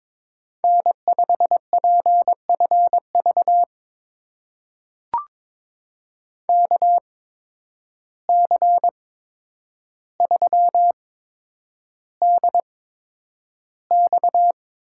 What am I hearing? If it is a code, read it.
Morse code, heard as a keyed tone, about 22 wpm, spaced as for 9 wpm, 700 Hz. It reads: N5PFV KC3DX
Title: USA Call Signs - 5 Characters - 8x Character Spacing - 22wpm